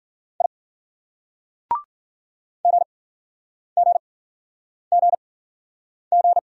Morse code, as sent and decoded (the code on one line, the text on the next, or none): .. / --. --. --. --.
I GGGG